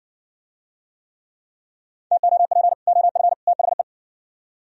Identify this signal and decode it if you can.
Morse code, heard as a keyed tone, about 69 wpm, 700 Hz. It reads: TOP OF THE